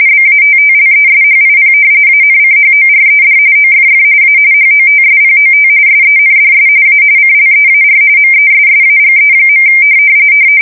RTTY, about 45.45 baud. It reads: BATTLESHIP ANCHORED OFF A HOTSPOT SENDS A CLEAR MESSAGE.
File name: fsk170 / battleshipnj